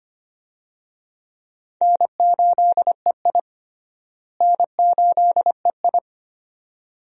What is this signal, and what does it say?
Morse code, heard as a keyed tone, about 25 wpm, 700 Hz. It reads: N8EI N8EI
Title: USA Call Signs - 4 Characters - Repeated 2x - 25wpm